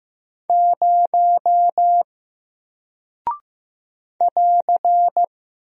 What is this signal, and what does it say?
Morse code, heard as a keyed tone, about 15 wpm, 700 Hz. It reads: 0 <AR>
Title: Single Character (Letter:Number:Punctuation:Prosign) 15wpm